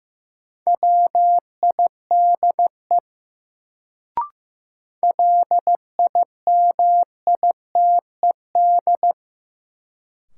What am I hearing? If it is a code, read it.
Morse code, heard as a keyed tone, about 15 wpm, 700 Hz. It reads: WIDE LIMITED
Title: Sentences from Top 900 Words - New Word Drill - 15wpm